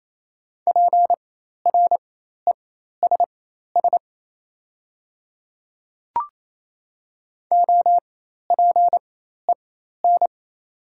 Morse code, heard as a keyed tone, about 28 wpm, spaced as for 13 wpm, 690 Hz. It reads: PRESS OPEN
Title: 4x Character Spacing - Top 300 Words - 28wpm